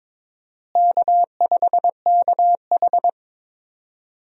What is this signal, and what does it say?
Morse code, heard as a keyed tone, about 22 wpm, 700 Hz. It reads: K5KH
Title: USA Call Signs - 4 Characters - Repeated 2x - 22wpm